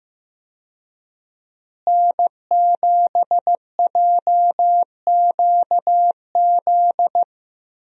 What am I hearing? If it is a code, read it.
Morse code, heard as a keyed tone, about 15 wpm, 700 Hz. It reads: N7JQZ
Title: USA Call Signs - 5 Characters - Repeated 3x - 15wpm